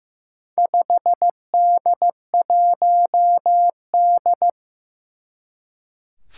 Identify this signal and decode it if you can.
Morse code, heard as a keyed tone, about 15 wpm, 700 Hz. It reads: 5D1D
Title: Four Challenging Letter-Number Combinations 15wpm